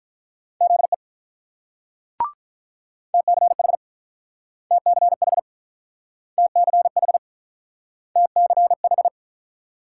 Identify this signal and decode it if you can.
Morse code, heard as a keyed tone, about 46 wpm, 700 Hz. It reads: ZE TCH TCH TCH TCH